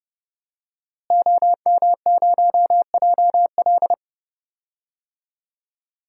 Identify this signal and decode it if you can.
Morse code, heard as a keyed tone, about 30 wpm, 700 Hz. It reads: OM0JL
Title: DXCC-100 Call Signs - 5 Characters - Repeated 3x - 30wpm